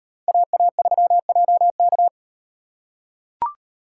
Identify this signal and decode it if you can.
Morse code, heard as a keyed tone, about 38 wpm, 700 Hz. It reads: AA3JK